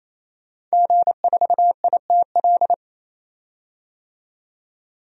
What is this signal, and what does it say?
Morse code, heard as a keyed tone, about 28 wpm, 700 Hz. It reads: G4ITL
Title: DXCC-100 Call Signs - 5 Characters - 28wpm